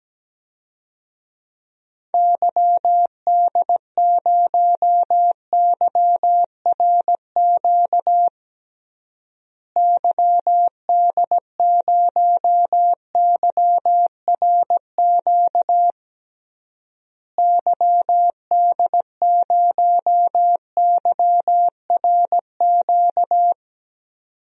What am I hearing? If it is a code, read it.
Morse code, heard as a keyed tone, about 17 wpm, 700 Hz. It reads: YD0YRQ YD0YRQ YD0YRQ
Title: DXCC-100 Call Signs - 6 Characters - Repeated 3x - 17wpm